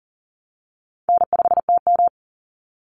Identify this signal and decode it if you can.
Morse code, heard as a keyed tone, about 40 wpm, 700 Hz. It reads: N5TM
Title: Easy Call Signs Repeated 3x - 40wpm